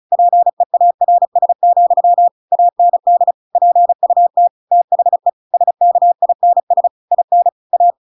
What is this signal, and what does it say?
Morse code, heard as a keyed tone, about 34 wpm, 700 Hz. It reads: PEARS, AND PUT THE SKINS IN A
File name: Pinocchio / Pinocchio ch7 35 35 ik1hge